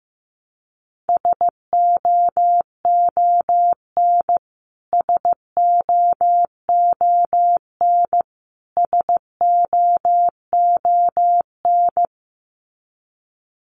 Morse code, heard as a keyed tone, about 15 wpm, 700 Hz. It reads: SOON SOON SOON
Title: Top 500 Words Repeated 15wpm